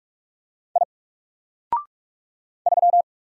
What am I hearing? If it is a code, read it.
Morse code, heard as a keyed tone, about 46 wpm, 700 Hz. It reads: I 3
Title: Single Character (Letter:Number:Punctuation:Prosign) 45wpm